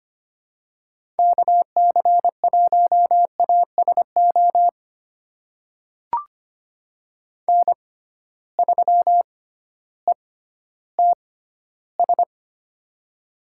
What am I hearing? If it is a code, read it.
Morse code, heard as a keyed tone, about 25 wpm, spaced as for 14 wpm, 700 Hz. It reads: KC1ASO N3ETS
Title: USA Call Signs - 6x Character Spacing - 25wpm